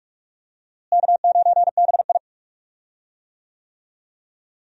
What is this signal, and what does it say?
Morse code, heard as a keyed tone, about 46 wpm, 700 Hz. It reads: K9BI
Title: USA Call Signs - 4 Characters - Repeated 2x - 45wpm